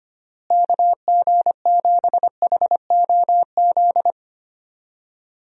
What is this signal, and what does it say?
Morse code, heard as a keyed tone, about 25 wpm, 700 Hz. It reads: KG7HOZ
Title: USA Call Signs - 6 Characters - Repeated 3x - 25wpm